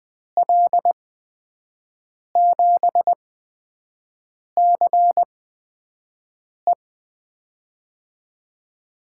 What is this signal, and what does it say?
Morse code, heard as a keyed tone, about 20 wpm, 700 Hz. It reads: L 7 C E
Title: USA Call Signs - 5 Characters - 8x Character Spacing - 20wpm